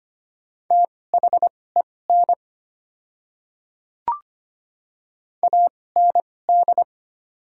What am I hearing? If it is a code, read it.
Morse code, heard as a keyed tone, about 25 wpm, spaced as for 18 wpm, 690 Hz. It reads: THEN AND
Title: 2x Character Spacing - Top 100 Words - 25wpm